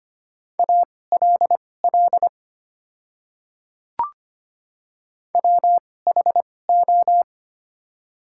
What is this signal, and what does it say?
Morse code, heard as a keyed tone, about 25 wpm, spaced as for 18 wpm, 700 Hz. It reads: ALL WHO
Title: 2x Character Spacing - Top 100 Words - 25wpm